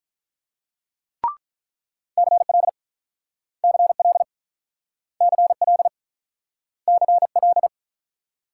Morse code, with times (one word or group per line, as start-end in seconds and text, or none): 2.2-2.7 CL
3.6-4.2 CL
5.2-5.9 CL
6.9-7.7 CL